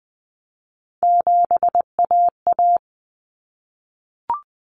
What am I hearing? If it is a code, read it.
Morse code, heard as a keyed tone, about 20 wpm, 700 Hz. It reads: 7AA
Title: Common Cut Numbers - 3-Digit 20wpm